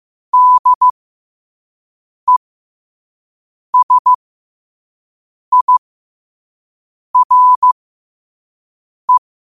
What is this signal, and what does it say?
Morse code, heard as a keyed tone, about 15 wpm, spaced as for 5 wpm, 1000 Hz. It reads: DESIRE
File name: qotdfiles / 5wpm